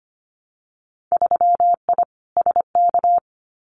Morse code, heard as a keyed tone, about 25 wpm, 700 Hz. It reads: 3I SK
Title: Winter Field Day Exchanges Encore 25wpm